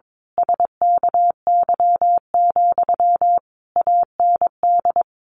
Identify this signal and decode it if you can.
Morse code, heard as a keyed tone, about 22 wpm, 700 Hz. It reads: SKY, AND
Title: A Sound Of Thunder 22wpm